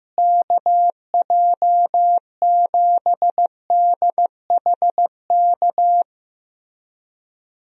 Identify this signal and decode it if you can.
Morse code, heard as a keyed tone, about 15 wpm, 700 Hz. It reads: KJ7DHK
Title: USA Call Signs - 6 Characters - Repeated 2x - 15wpm